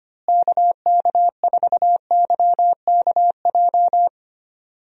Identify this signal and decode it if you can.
Morse code, heard as a keyed tone, about 25 wpm, 700 Hz. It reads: KK4YKJ
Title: USA Call Signs - 6 Characters - Repeated 2x - 25wpm